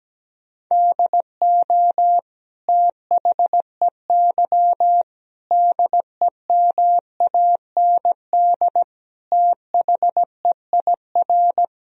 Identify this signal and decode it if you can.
Morse code, heard as a keyed tone, about 17 wpm, 700 Hz. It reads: DO THEY DEMAND THEIR
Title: Sentences from Top 1000 Words - Review Part 3 - 17wpm